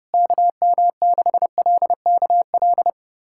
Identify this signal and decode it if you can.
Morse code, heard as a keyed tone, about 29 wpm, 700 Hz. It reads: KM6LKL
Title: USA Callsign DE Callsign - 6 Characters - 30wpm